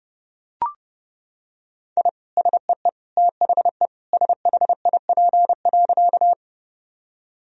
Morse code, heard as a keyed tone, about 30 wpm, 700 Hz. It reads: I SEE THE SHIP.
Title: Sentences from Top 1000 Words - Review Part 2 - 30wpm